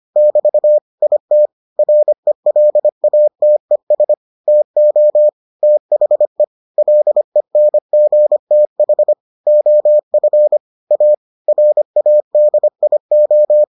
Morse code, heard as a keyed tone, about 25 wpm, 600 Hz. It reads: <BT> IT RELATES TO THE LENGTH OF A RADIO